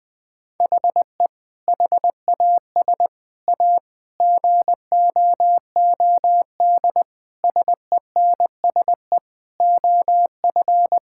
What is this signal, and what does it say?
Morse code, heard as a keyed tone, about 21 wpm, 700 Hz. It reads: HE HAS A GOOD SENSE OF
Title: Sentences with 2-Word Phrases - 20wpm